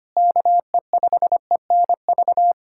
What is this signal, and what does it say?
Morse code, heard as a keyed tone, about 25 wpm, 700 Hz. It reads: KE5ENV
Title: USA Callsign DE Callsign - 6 Characters - 25wpm